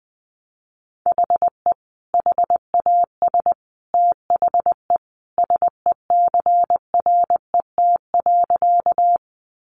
Morse code, heard as a keyed tone, about 20 wpm, 700 Hz. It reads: HE HAS THE SECRET.
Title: Sentences from Top 1200 Words 20wpm